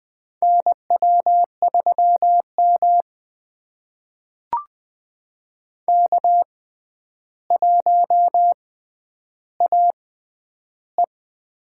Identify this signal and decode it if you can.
Morse code, heard as a keyed tone, about 20 wpm, spaced as for 10 wpm, 700 Hz. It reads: NW3M K1AE
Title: USA Call Signs - 4 Characters - 6x Character Spacing - 20wpm